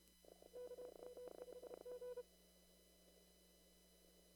Morse code, heard as a keyed tone, about 32 wpm, 500 Hz. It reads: KM5G
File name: CW / kt5j wpx cw-op-k5pi